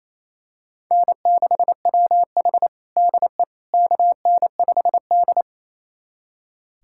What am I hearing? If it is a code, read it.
Morse code, heard as a keyed tone, about 28 wpm, 700 Hz. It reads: N6WH DE KN5D
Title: USA Callsign DE Callsign - 4 Characters - 28wpm